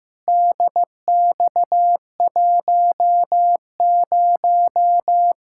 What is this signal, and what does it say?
Morse code, heard as a keyed tone, about 16 wpm, 700 Hz. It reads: DX10
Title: Practice QSO - Exchange 2 - Info - 15wpm